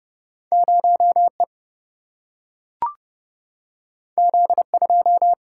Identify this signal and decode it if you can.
Morse code, heard as a keyed tone, about 30 wpm, 700 Hz. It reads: 0 E Z 2
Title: Single-Letter Single-Number 30wpm